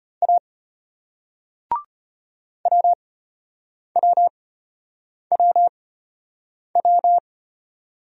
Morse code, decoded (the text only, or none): A WWWW